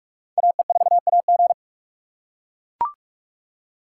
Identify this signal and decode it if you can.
Morse code, heard as a keyed tone, about 40 wpm, 700 Hz. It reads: AE4AG